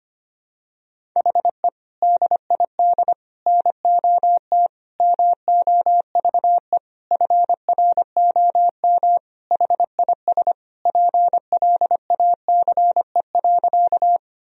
Morse code, heard as a keyed tone, about 25 wpm, 700 Hz. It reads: HE DID NOT MOVE FROM HIS PLACE.
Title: Sentences from Top 1000 Words - Review Part 1 - 25wpm